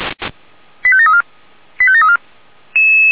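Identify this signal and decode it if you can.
Five-tone selective call (ZVEI), heard as quick runs of five tones, tones of about 70 ms. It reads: 85734 85734
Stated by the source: ZVEI five-tone selective call received on an SDR.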